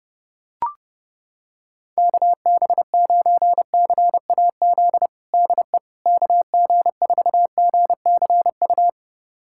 Morse code, heard as a keyed tone, about 30 wpm, 700 Hz. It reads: KB9CAZ DE KG4GCU
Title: USA Callsign DE Callsign 30wpm